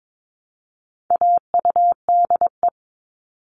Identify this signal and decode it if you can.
Morse code, heard as a keyed tone, about 22 wpm, 700 Hz. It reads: AUDE
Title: All Cut Numbers - 4-Digit 22wpm